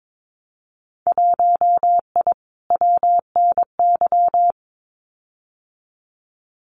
Morse code, heard as a keyed tone, about 22 wpm, 700 Hz. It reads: 1I WNY
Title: Winter Field Day Exchanges Encore 22wpm